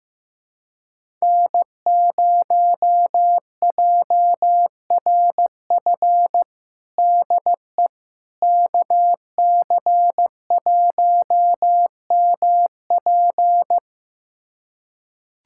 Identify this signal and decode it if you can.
Morse code, heard as a keyed tone, about 15 wpm, 700 Hz. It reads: N0JRF DE KC1MP
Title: USA Callsign DE Callsign - 5 Characters - 15wpm